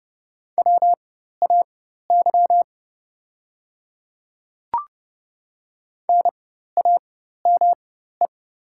Morse code, heard as a keyed tone, about 30 wpm, spaced as for 14 wpm, 700 Hz. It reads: WAY NAME
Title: 4x Character Spacing - Top 200 Words - 30wpm